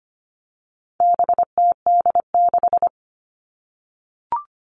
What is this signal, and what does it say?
Morse code, heard as a keyed tone, about 25 wpm, 700 Hz. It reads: BTD6
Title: All Cut Numbers - 4-Digit 25wpm